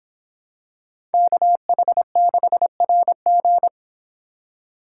Morse code, heard as a keyed tone, about 26 wpm, 700 Hz. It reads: KH6RG